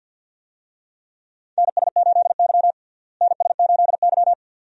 Morse code, heard as a keyed tone, about 52 wpm, 700 Hz. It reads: NS8Y NS8Y
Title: USA Call Signs - 4 Characters - Repeated 2x - 50wpm